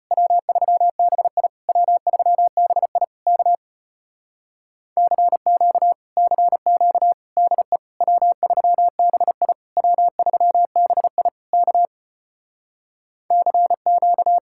Morse code, heard as a keyed tone, about 33 wpm, 700 Hz. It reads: W3BI W3BI K CQ CQ DE W3BI W3BI K CQ